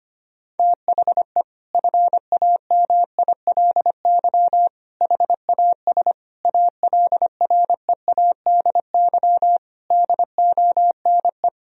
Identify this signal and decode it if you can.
Morse code, heard as a keyed tone, about 25 wpm, 700 Hz. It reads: THE FAMILY HAS ALREADY DONE
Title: Sentences from Top 1000 Words - Review Part 2 - 25wpm